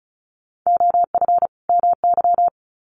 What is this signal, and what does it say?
Morse code, heard as a keyed tone, about 34 wpm, 700 Hz. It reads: OF MY
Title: Sets of 2 Words 35wpm